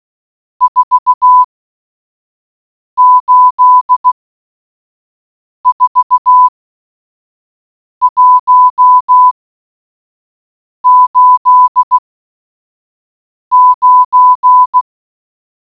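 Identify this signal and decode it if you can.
Morse code, heard as a keyed tone, about 16 wpm, 1000 Hz. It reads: 4 8 4 1 8 9
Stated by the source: Morse audio clue in a Codebreaker Portal puzzle game, AIT Forensics Dashboard.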